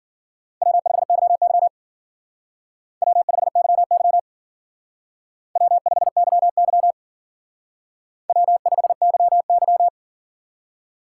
Morse code, heard as a keyed tone, about 51 wpm, 700 Hz. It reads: W5YY W5YY W5YY W5YY